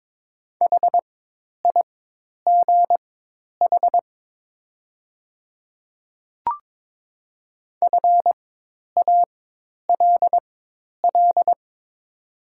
Morse code, heard as a keyed tone, about 22 wpm, spaced as for 10 wpm, 700 Hz. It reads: HIGH FALL
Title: 4x Character Spacing - Top 500 Words - 22wpm